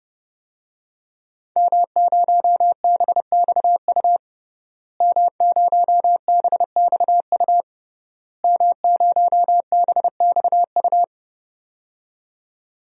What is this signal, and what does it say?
Morse code, heard as a keyed tone, about 30 wpm, 700 Hz. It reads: M0BXU M0BXU M0BXU
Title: DXCC-100 Call Signs - 5 Characters - Repeated 3x - 30wpm